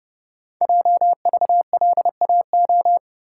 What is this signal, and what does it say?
Morse code, heard as a keyed tone, about 30 wpm, 700 Hz. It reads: JVLAO
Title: Code Groups - 5 Letter:Numbers - 30wpm